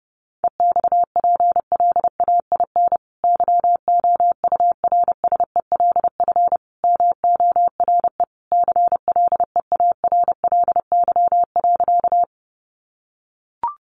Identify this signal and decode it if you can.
Morse code, heard as a keyed tone, about 30 wpm, 700 Hz. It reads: EXPLAIN YOURSELF MORE CLEARLY.
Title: Sentences from Top 1300 Words 30wpm